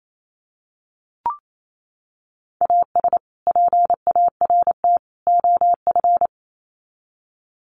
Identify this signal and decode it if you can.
Morse code, heard as a keyed tone, about 28 wpm, 700 Hz. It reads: AS PART OF
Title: Sets of 3 Words 28wpm